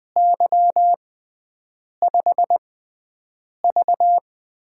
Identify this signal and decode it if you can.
Morse code, heard as a keyed tone, about 20 wpm, spaced as for 7 wpm, 700 Hz. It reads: Y5V
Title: USA Call Signs - 4 Characters - 6x Character Spacing - 20wpm